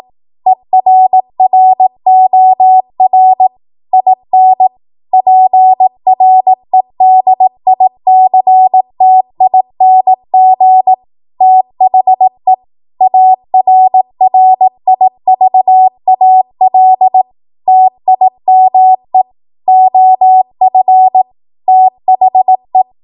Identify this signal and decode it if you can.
Morse code, heard as a keyed tone, about 18 wpm, 750 Hz. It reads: ERROR IN PREDICTING THE ARRIVAL TIME OF THE
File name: Morse / 18WPM